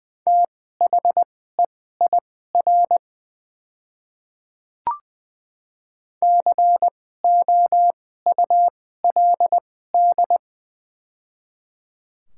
Morse code, heard as a keyed tone, about 20 wpm, spaced as for 14 wpm, 700 Hz. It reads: THEIR COULD
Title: 2x Character Spacing - Top 500 Words - 20wpm